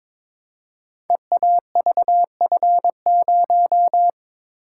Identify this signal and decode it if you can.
Morse code, heard as a keyed tone, about 22 wpm, 700 Hz. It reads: EAVF0
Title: Code Groups - 5 Letter:Numbers - 22wpm